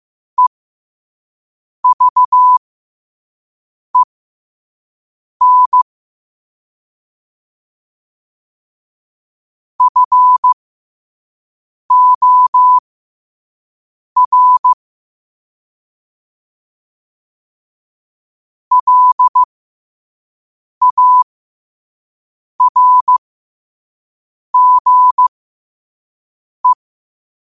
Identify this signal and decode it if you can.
Morse code, heard as a keyed tone, about 15 wpm, spaced as for 5 wpm, 1000 Hz. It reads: EVEN FOR LARGE
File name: qotdfiles / 5wpm